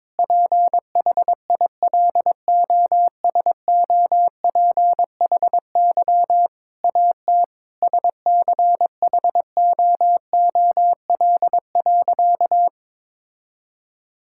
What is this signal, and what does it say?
Morse code, heard as a keyed tone, about 22 wpm, 700 Hz. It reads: PHILOSOPHY AT SCHOOL.
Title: Sentences from Top 1000 Words - Review Part 6 - 22wpm